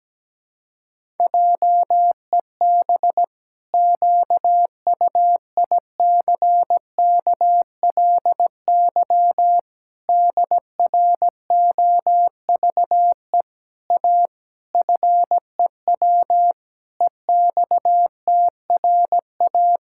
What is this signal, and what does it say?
Morse code, heard as a keyed tone, about 17 wpm, 700 Hz. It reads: JEB QUICKLY DROVE A FEW EXTRA
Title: Pangram Challenge v2 - 17wpm